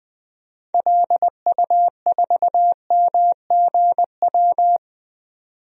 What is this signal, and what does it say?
Morse code, heard as a keyed tone, about 20 wpm, 700 Hz. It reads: LU4MGW
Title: DXCC-100 Call Signs - 6 Characters - 20wpm